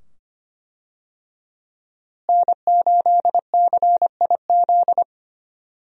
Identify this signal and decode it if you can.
Morse code, heard as a keyed tone, about 25 wpm, 700 Hz. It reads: N8CIZ
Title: USA Call Signs - 5 Characters - Repeated 2x - 25wpm